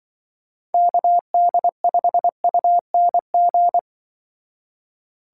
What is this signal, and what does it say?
Morse code, heard as a keyed tone, about 24 wpm, 700 Hz. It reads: KD5UNG
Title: Speed Racing - USA 2x3 Call Signs 20wpm